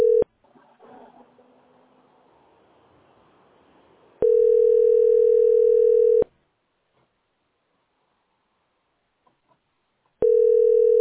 Ringback tone (North America), 440 + 480 Hz.